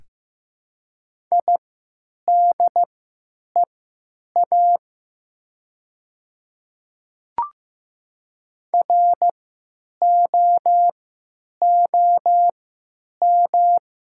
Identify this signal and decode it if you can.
Morse code, heard as a keyed tone, about 15 wpm, spaced as for 8 wpm, 700 Hz. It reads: IDEA ROOM
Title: 3x Character Spacing - Top 300 Words - 15wpm